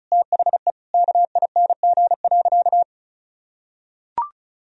Morse code, heard as a keyed tone, about 36 wpm, 700 Hz. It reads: THE KING.
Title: Sentences from Top 1000 Words - Review Part 4 - 35wpm